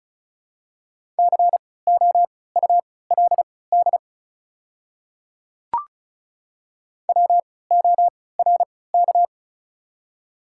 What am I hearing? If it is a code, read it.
Morse code, heard as a keyed tone, about 36 wpm, spaced as for 20 wpm, 700 Hz. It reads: COULD WORK
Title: 3x Character Spacing - Top 300 Words - 35wpm